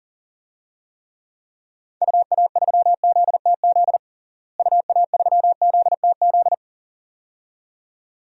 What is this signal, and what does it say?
Morse code, heard as a keyed tone, about 40 wpm, 700 Hz. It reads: UA3ZTZ UA3ZTZ
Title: DXCC-100 Call Signs - 6 Characters - Repeated 2x - 40wpm